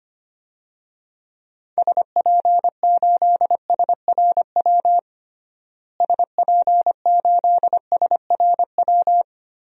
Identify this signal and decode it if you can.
Morse code, heard as a keyed tone, about 25 wpm, 700 Hz. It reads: SP8SRW SP8SRW
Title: DXCC-100 Call Signs - 6 Characters - Repeated 3x - 25wpm